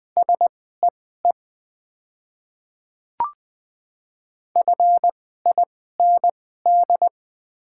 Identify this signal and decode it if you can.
Morse code, heard as a keyed tone, about 20 wpm, spaced as for 15 wpm, 690 Hz. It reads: SEE FIND
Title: 2x Character Spacing - Top 100 Words - 20wpm